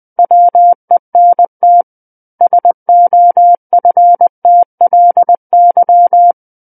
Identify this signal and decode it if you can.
Morse code, heard as a keyed tone, about 20 wpm, 700 Hz. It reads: WENT SOFTLY
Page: SKCC Learning Center - Morse Code Practice Files